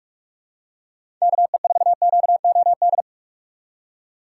Morse code, heard as a keyed tone, about 46 wpm, 700 Hz. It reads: KE4QOD
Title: USA Call Signs - 3x Character Spacing - 45wpm